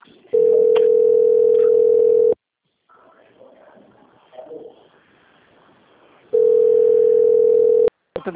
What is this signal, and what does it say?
Ringback tone (North America), 440 + 480 Hz.